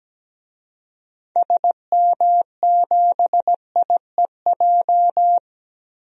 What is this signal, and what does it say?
Morse code, heard as a keyed tone, about 17 wpm, 700 Hz. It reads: SM7IEJ
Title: DXCC-100 Call Signs - 6 Characters - 17wpm